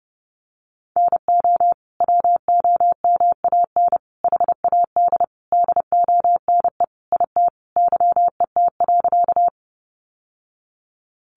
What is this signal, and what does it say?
Morse code, heard as a keyed tone, about 28 wpm, 700 Hz. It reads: NO WOMAN HAD DONE IT YET.
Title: Sentences from Top 1100 Words 30wpm